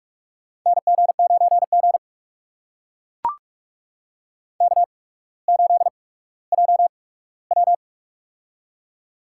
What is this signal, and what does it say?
Morse code, heard as a keyed tone, about 44 wpm, spaced as for 19 wpm, 700 Hz. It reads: NG9G K8JW
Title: USA Call Signs - 4 Characters - 8x Character Spacing - 45wpm